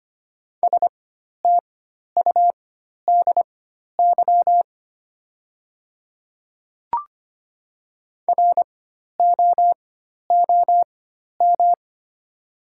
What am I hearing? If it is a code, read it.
Morse code, heard as a keyed tone, about 25 wpm, spaced as for 12 wpm, 700 Hz. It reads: STUDY ROOM
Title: 4x Character Spacing - Top 300 Words - 25wpm